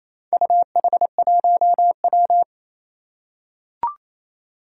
Morse code, heard as a keyed tone, about 28 wpm, 700 Hz. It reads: UH1W
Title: Four Challenging Letter-Number Combinations 28wpm